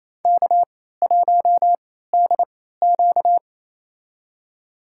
Morse code, heard as a keyed tone, about 28 wpm, spaced as for 16 wpm, 700 Hz. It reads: K1DQ
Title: USA Call Signs - 4 Characters - 3x Character Spacing - 28wpm